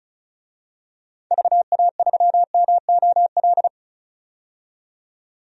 Morse code, heard as a keyed tone, about 36 wpm, 700 Hz. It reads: VA3MOL
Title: DXCC-100 Call Signs - 6 Characters - 35wpm